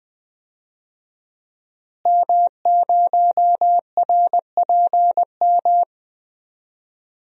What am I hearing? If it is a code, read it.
Morse code, heard as a keyed tone, about 20 wpm, 700 Hz. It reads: M0RPM
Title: DXCC-100 Call Signs - 5 Characters - Repeated 3x - 20wpm